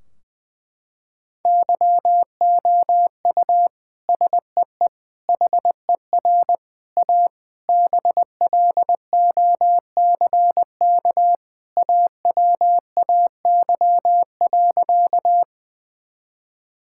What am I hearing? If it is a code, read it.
Morse code, heard as a keyed tone, about 20 wpm, 700 Hz. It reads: YOU SEE HER A BLOCK AWAY.
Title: Sentences from Top 1600 Words 20wpm